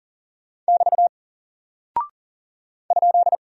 Morse code, heard as a keyed tone, about 40 wpm, 700 Hz. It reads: <BT> ?
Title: Single Character (Letter:Number:Punctuation:Prosign) Alt 40wpm